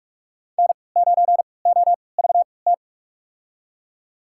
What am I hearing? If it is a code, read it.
Morse code, heard as a keyed tone, about 43 wpm, spaced as for 25 wpm, 700 Hz. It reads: N9OVT
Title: USA Call Signs - 5 Characters - 3x Character Spacing - 45wpm